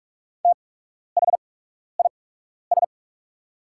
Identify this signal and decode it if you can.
Morse code, heard as a keyed tone, about 48 wpm, spaced as for 12 wpm, 690 Hz. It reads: THIS
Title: 8x Character Spacing - Top 200 Words - 45wpm